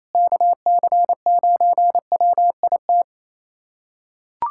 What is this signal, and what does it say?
Morse code, heard as a keyed tone, about 28 wpm, 700 Hz. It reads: KC9WIT
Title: USA Callsign DE Callsign 28wpm